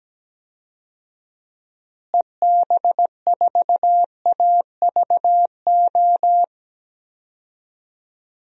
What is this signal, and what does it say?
Morse code, heard as a keyed tone, about 17 wpm, 700 Hz. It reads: EB4AVO
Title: DXCC-100 Call Signs - 6 Characters - Repeated 3x - 17wpm